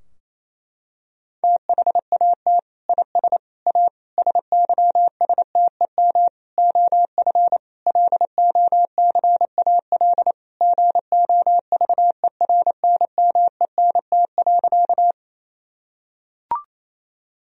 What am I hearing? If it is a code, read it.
Morse code, heard as a keyed tone, about 28 wpm, 700 Hz. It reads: THAT IS A SYSTEM OF LOCAL GOVERNMENT.
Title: Sentences from Top 1000 Words - Review Part 3 - 28wpm